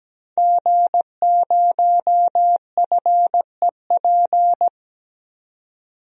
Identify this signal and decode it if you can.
Morse code, heard as a keyed tone, about 17 wpm, 700 Hz. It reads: G0FEP
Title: DXCC-100 Call Signs - 5 Characters - 17wpm